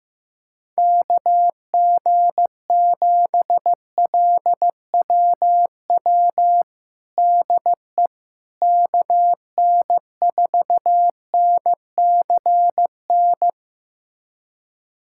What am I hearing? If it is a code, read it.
Morse code, heard as a keyed tone, about 16 wpm, 700 Hz. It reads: KG7LWW DE KN4NCN
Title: USA Callsign DE Callsign 15wpm